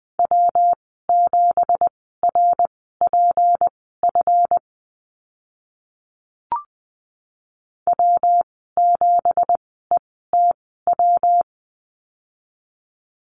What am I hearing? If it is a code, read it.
Morse code, heard as a keyed tone, about 20 wpm, spaced as for 14 wpm, 700 Hz. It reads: W7RPF W7ETW
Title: 173 Lesson 28g - Number 9 - Call Signs 20wpm